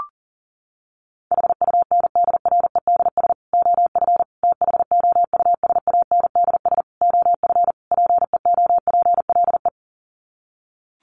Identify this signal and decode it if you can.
Morse code, heard as a keyed tone, about 40 wpm, 700 Hz. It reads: HUNDREDS OF THOUSANDS OF PEOPLE
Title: Sets of 5 Words Encore2 40wpm